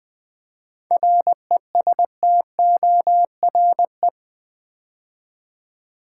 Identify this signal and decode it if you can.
Morse code, heard as a keyed tone, about 20 wpm, 700 Hz. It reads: RESTORE
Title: 7 Letter Words - First 3 Letters Given - 20wpm